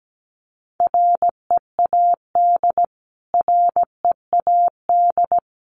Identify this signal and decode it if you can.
Morse code, heard as a keyed tone, about 17 wpm, 700 Hz. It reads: READ READ
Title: Top 500 Words Repeated 17wpm